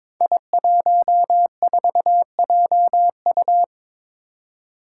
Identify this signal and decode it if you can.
Morse code, heard as a keyed tone, about 22 wpm, 700 Hz. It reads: I14JU
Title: Five Challenging Letter-Number Combinations 22wpm